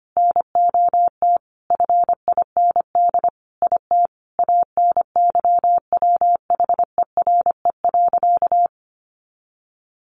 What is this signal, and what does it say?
Morse code, heard as a keyed tone, about 25 wpm, 700 Hz. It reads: NOT FIND IT ANYWHERE.
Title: Sentences from Top 1400 Words 25wpm